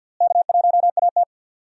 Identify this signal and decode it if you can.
Morse code, heard as a keyed tone, about 53 wpm, 700 Hz. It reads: K1AT
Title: USA Call Signs - 4 Characters - Repeated 3x - 50wpm